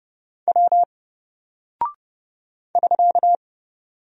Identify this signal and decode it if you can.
Morse code, heard as a keyed tone, about 29 wpm, 700 Hz. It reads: W <SK>
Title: Single Character (Letter:Number:Punctuation:Prosign) 30wpm